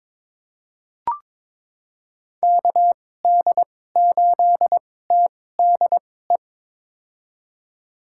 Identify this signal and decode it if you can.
Morse code, heard as a keyed tone, about 22 wpm, spaced as for 16 wpm, 700 Hz. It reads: KD8TDE
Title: USA Call Signs - 6 Characters - 2x Character Spacing - 22wpm